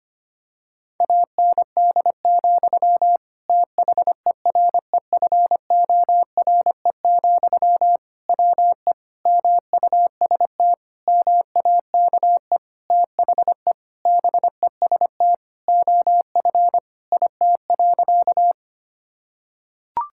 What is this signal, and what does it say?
Morse code, heard as a keyed tone, about 25 wpm, 700 Hz. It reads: AND, THEREFORE, WE MUST MAKE THE BEST OF IT.
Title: Sentences from Top 1000 Words - Review Part 5 - 25wpm